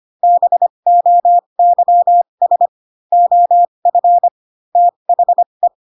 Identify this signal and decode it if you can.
Morse code, heard as a keyed tone, about 25 wpm, spaced as for 22 wpm, 700 Hz. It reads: BOYS OF THE
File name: Pinocchio / Pinocchio ch2 18 25 ik1hge